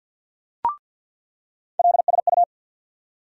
Morse code, heard as a keyed tone, about 46 wpm, 700 Hz. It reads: LSU
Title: USA Call Sign Postfixes - 3 Letters 50wpm